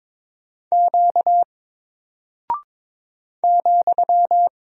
Morse code, heard as a keyed tone, about 22 wpm, 700 Hz. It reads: Q ,
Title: Single Character (Letter:Number:Punctuation:Prosign) Alt 22wpm